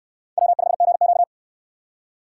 Morse code, heard as a keyed tone, about 63 wpm, 700 Hz. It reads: W5GQ